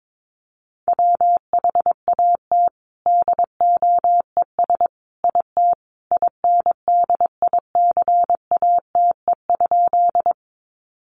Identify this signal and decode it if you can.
Morse code, heard as a keyed tone, about 22 wpm, 700 Hz. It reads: WHAT DOES IT INDICATE?
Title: Sentences from Top 1300 Words 22wpm